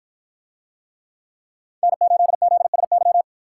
Morse code, heard as a keyed tone, about 53 wpm, 700 Hz. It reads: N8ZSY